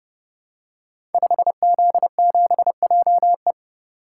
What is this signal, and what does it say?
Morse code, heard as a keyed tone, about 30 wpm, 700 Hz. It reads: 5Z7JE
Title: Code Groups - 5 Letter:Numbers - 30wpm